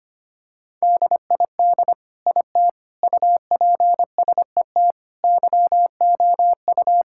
Morse code, heard as a keyed tone, about 25 wpm, 700 Hz. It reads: DID IT UPSET YOU
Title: Sentences from Top 1000 Words - Review Part 5 - 25wpm